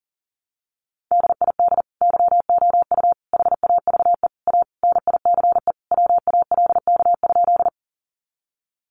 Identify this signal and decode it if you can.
Morse code, heard as a keyed tone, about 40 wpm, 700 Hz. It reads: DID YOU HAVE A NICE WALK?
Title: Sentences from Top 1100 Words 40wpm